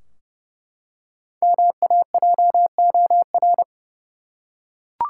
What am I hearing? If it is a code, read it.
Morse code, heard as a keyed tone, about 30 wpm, 700 Hz. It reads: MAJOR
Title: Five Letter Words - 30wpm